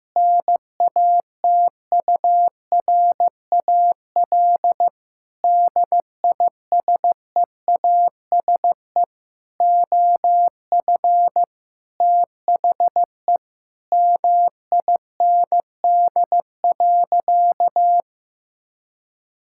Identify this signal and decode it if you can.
Morse code, heard as a keyed tone, about 16 wpm, 700 Hz. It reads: NATURAL DISEASE OF THE MIND.
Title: Sentences from Top 1000 Words - Review Part 6 - 15wpm